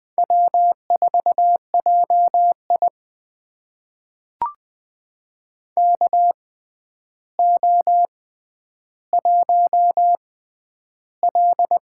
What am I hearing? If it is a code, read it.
Morse code, heard as a keyed tone, about 20 wpm, spaced as for 10 wpm, 700 Hz. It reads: W4JI KO1L
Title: USA Call Signs - 4 Characters - 6x Character Spacing - 20wpm